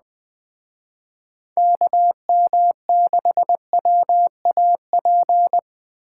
Morse code, heard as a keyed tone, about 20 wpm, 700 Hz. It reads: KM6WAP